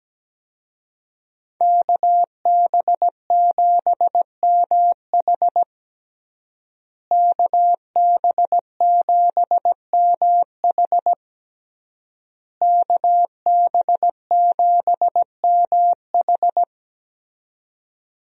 Morse code, heard as a keyed tone, about 17 wpm, 700 Hz. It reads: KB7MH KB7MH KB7MH
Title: USA Call Signs - 5 Characters - Repeated 3x - 17wpm